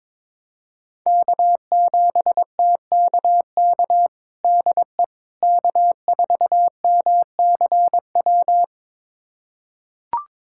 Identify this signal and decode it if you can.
Morse code, heard as a keyed tone, about 21 wpm, 700 Hz. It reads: K7TKK DE K4MCW
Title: USA Callsign DE Callsign - 5 Characters - 22wpm